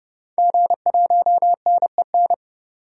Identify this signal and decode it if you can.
Morse code, heard as a keyed tone, about 30 wpm, 700 Hz. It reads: G1NEN
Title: DXCC-100 Call Signs - 5 Characters - 30wpm